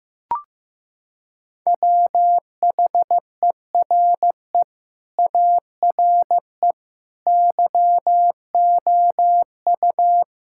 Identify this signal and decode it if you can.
Morse code, heard as a keyed tone, about 15 wpm, 700 Hz. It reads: WHERE ARE YOU
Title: Sentences from Top 1600 Words 15wpm